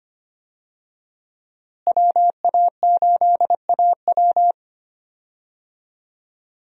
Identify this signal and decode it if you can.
Morse code, heard as a keyed tone, about 25 wpm, 700 Hz. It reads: WA8AW
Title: USA Call Signs - 5 Characters - Repeated 2x - 25wpm